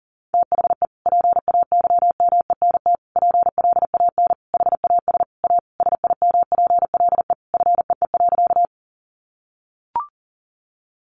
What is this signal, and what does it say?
Morse code, heard as a keyed tone, about 40 wpm, 700 Hz. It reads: THE PAYMENT PLAN HAS A SIMPLE FEE.
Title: Sentences from Top 1500 Words 40wpm